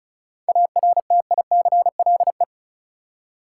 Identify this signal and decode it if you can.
Morse code, heard as a keyed tone, about 36 wpm, 700 Hz. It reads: ARTICLE
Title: Seven Letter Words - 35wpm